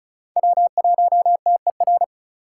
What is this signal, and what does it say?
Morse code, heard as a keyed tone, about 36 wpm, 700 Hz. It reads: W1TER
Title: K1USN SST QSO Practice - 35wpm